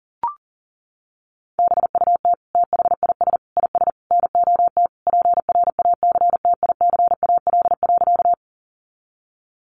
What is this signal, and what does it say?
Morse code, heard as a keyed tone, about 40 wpm, 700 Hz. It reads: BUT THIS IS NOT PRACTICAL.
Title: Sentences from Top 1400 Words 40wpm